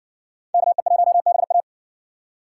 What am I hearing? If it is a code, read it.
Morse code, heard as a keyed tone, about 60 wpm, 700 Hz. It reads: KE1BA